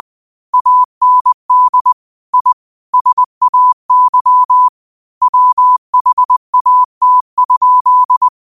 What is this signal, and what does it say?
Morse code, heard as a keyed tone, about 20 wpm, 1000 Hz. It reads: AND I SAY WHAT?
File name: qotdfiles / 20wpm